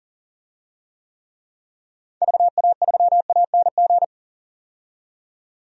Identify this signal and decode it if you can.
Morse code, heard as a keyed tone, about 39 wpm, 700 Hz. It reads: VA3ANG
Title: DXCC-100 Call Signs - 6 Characters - 40wpm